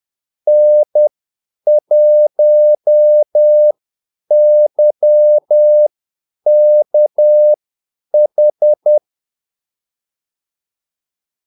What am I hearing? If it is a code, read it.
Morse code, heard as a keyed tone, about 10 wpm, spaced as for 8 wpm, 600 Hz. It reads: N1YKH
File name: Callsigns-10-8wpm